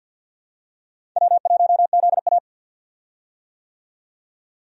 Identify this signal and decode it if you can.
Morse code, heard as a keyed tone, about 46 wpm, 700 Hz. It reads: W1ZA
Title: USA Call Signs - 4 Characters - 50wpm